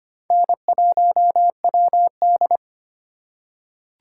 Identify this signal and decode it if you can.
Morse code, heard as a keyed tone, about 25 wpm, 700 Hz. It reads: N1WD
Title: USA Call Signs - 4 Characters - 2x Character Spacing - 25wpm